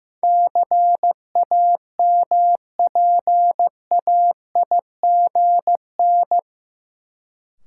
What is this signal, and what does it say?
Morse code, heard as a keyed tone, about 16 wpm, 700 Hz. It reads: CAMPAIGN
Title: Sentences from Top 1200 Words - New Word Drill - 15wpm